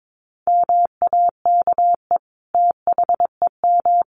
Morse code, heard as a keyed tone, about 22 wpm, 700 Hz. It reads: MAKE THEM
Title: Sentences from Top 900 Words 22wpm